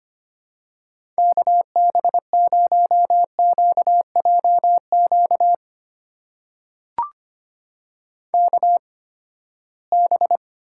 Morse code, heard as a keyed tone, about 25 wpm, spaced as for 17 wpm, 700 Hz. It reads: KB0QJQ KB
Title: USA Call Signs - 6 Characters - 8x Character Spacing - 25wpm